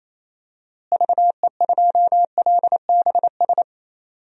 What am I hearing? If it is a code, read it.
Morse code, heard as a keyed tone, about 28 wpm, 700 Hz. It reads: VE2LBS
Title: DXCC-100 Call Signs - 6 Characters - Repeated 2x - 28wpm